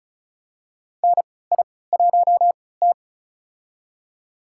Morse code, heard as a keyed tone, about 34 wpm, spaced as for 20 wpm, 700 Hz. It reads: NI1T
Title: USA Call Signs - 3x Character Spacing - 35wpm